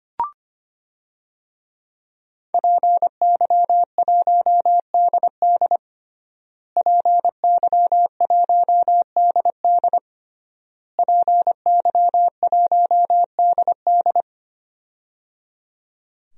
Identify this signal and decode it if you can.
Morse code, heard as a keyed tone, about 25 wpm, 700 Hz. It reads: PY1DD PY1DD PY1DD
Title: DXCC-100 Call Signs - 5 Characters - Repeated 3x - 25wpm